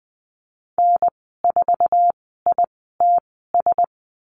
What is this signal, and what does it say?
Morse code, heard as a keyed tone, about 20 wpm, spaced as for 14 wpm, 700 Hz. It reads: N4ITS
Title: 048 Lesson 9g - Number 4 - Call Signs 20wpm